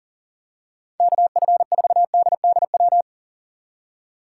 Morse code, heard as a keyed tone, about 40 wpm, 700 Hz. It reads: KF4DDW
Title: USA Call Signs - 6 Characters - 6x Character Spacing - 40wpm